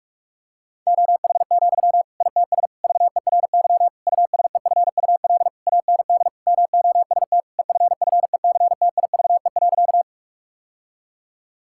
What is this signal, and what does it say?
Morse code, heard as a keyed tone, about 46 wpm, 700 Hz. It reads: OH, ITS VERY USEFUL AND MOST EFFECTIVE.
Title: Sentences from Top 1000 Words - Review Part 6 - 45wpm